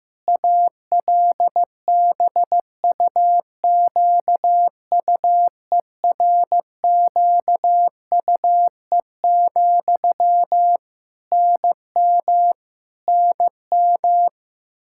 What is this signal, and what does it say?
Morse code, heard as a keyed tone, about 15 wpm, 700 Hz. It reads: ALBUQUERQUE, NM NM
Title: Practice QSO - Exchange 1 - Beginning - 15wpm